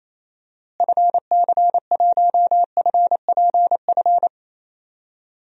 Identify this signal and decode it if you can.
Morse code, heard as a keyed tone, about 27 wpm, 700 Hz. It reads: FC1FPF
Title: DXCC-100 Call Signs - 6 Characters - Repeated 3x - 28wpm